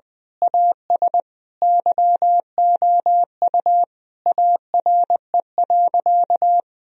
Morse code, heard as a keyed tone, about 19 wpm, 700 Hz. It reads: AS YOU ARE.
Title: Sentences from Top 1000 Words - Review Part 4 - 20wpm